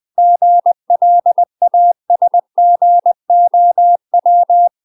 Morse code, heard as a keyed tone, about 20 wpm, 700 Hz. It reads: GLASGOW
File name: UK MASTER QSOs TRAINING / 20WPM / UK MASTER QSOs 0028